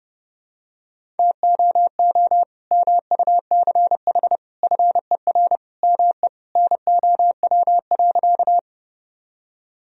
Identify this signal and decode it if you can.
Morse code, heard as a keyed tone, about 32 wpm, 700 Hz. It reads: TOO MUCH FER ME NOW.
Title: Sentences from Top 200 Words and Abbreviations 30wpm